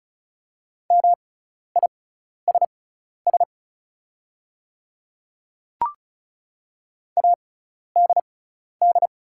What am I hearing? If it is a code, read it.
Morse code, heard as a keyed tone, about 36 wpm, spaced as for 12 wpm, 700 Hz. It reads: MISS ADD
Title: 6x Character Spacing - Top 500 Words - 35wpm